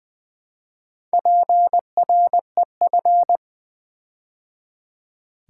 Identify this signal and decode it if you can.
Morse code, heard as a keyed tone, about 20 wpm, 700 Hz. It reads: PREF
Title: Build A Word - 4 Letters At A Time - 10-Letter Words - 20wpm